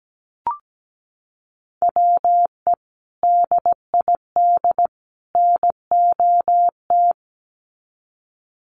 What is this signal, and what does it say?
Morse code, heard as a keyed tone, about 17 wpm, 700 Hz. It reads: WE DID NOT
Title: Sets of 3 Words 17wpm